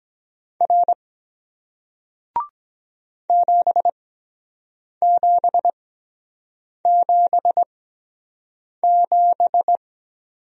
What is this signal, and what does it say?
Morse code, heard as a keyed tone, about 23 wpm, spaced as for 7 wpm, 700 Hz. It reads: R 7777